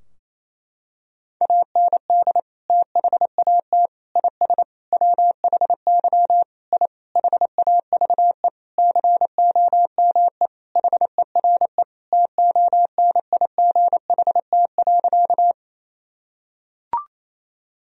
Morse code, heard as a keyed tone, about 28 wpm, 700 Hz. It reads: AND THAT IS WHY I HAVE COME HERE TONIGHT.
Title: Sentences from Top 1000 Words - Review Part 3 - 28wpm